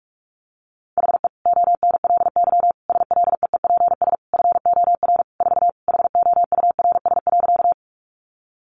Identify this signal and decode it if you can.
Morse code, heard as a keyed tone, about 44 wpm, 700 Hz. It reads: HE ONLY SLEEPS FOR 4 HOURS.
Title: Additional Easy Phrases 45wpm